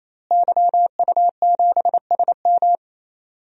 Morse code, heard as a keyed tone, about 27 wpm, 700 Hz. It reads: YU7SM
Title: DXCC-100 Call Signs - 5 Characters - Repeated 3x - 28wpm